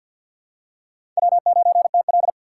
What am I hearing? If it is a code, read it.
Morse code, heard as a keyed tone, about 49 wpm, 700 Hz. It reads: W9TL
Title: USA Call Signs - 4 Characters - 6x Character Spacing - 50wpm